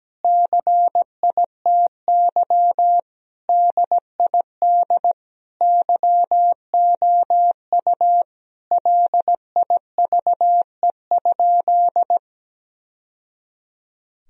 Morse code, heard as a keyed tone, about 17 wpm, 700 Hz. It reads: CITY DID YOU LIVE?
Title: Sentences from Top 1000 Words - Review Part 1 - 17wpm